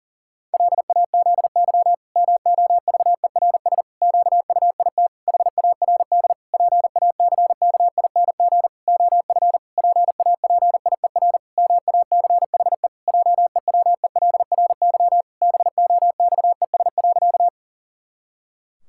Morse code, heard as a keyed tone, about 40 wpm, 700 Hz. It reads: LAZY MOVERS QUIT HARD PACKING OF PAPIER MACHE JEWELRY BOXES.
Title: Pangram Challenge v2 - 40wpm